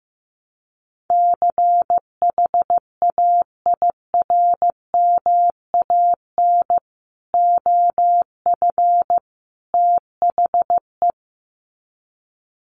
Morse code, heard as a keyed tone, about 15 wpm, 700 Hz. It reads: CHAIRMAN OF THE
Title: Sets of 3 Words 15wpm